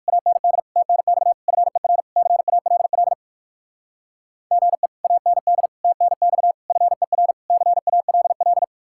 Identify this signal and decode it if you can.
Morse code, heard as a keyed tone, about 50 wpm, 700 Hz. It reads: AND TNX FER CALL GE AND TNX FER CALL
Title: Speed-Racing Common QSO Phrases Encore - 45wpm